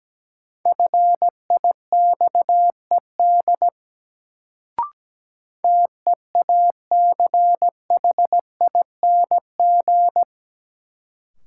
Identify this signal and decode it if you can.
Morse code, heard as a keyed tone, about 17 wpm, 700 Hz. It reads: FIXED TEACHING
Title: Sentences from Top 1100 Words - New Word Drill - 17wpm